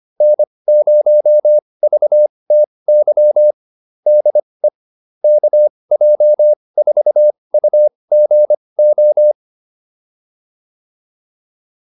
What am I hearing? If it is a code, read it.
Morse code, heard as a keyed tone, about 25 wpm, spaced as for 20 wpm, 600 Hz. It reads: N0VTY DE KJ4UGO